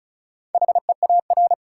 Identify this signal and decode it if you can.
Morse code, heard as a keyed tone, about 35 wpm, 700 Hz. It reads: HEAR
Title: Sentences from Top 300 Words - New Word Drill - 35wpm